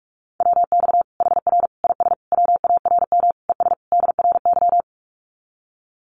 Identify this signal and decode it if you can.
Morse code, heard as a keyed tone, about 44 wpm, 700 Hz. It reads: WX HR IS WARM ES DRY
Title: Common QSO Phrases 45wpm